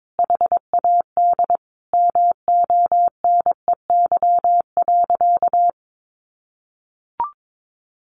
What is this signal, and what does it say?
Morse code, heard as a keyed tone, about 22 wpm, 700 Hz. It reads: HAD MONEY.
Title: Sentences from Top 500 Words 22wpm